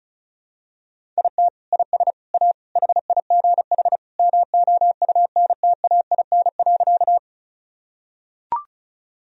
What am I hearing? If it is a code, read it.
Morse code, heard as a keyed tone, about 36 wpm, 700 Hz. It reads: IT IS A HIGH MOUNTAIN.
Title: Sentences from Top 1000 Words - Review Part 1 - 35wpm